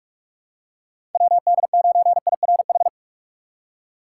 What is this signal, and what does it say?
Morse code, heard as a keyed tone, about 46 wpm, 700 Hz. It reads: WD9IRH
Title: USA Call Signs - 2x Character Spacing - 45wpm